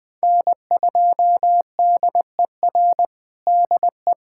Morse code, heard as a keyed tone, about 20 wpm, 700 Hz. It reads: N2DER DE
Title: USA Callsign DE Callsign - 5 Characters - 20wpm